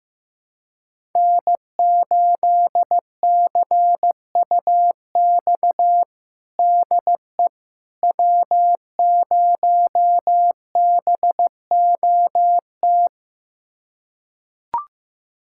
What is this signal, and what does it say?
Morse code, heard as a keyed tone, about 15 wpm, 700 Hz. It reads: N8CUX DE W0BOT
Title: USA Callsign DE Callsign - 5 Characters - 15wpm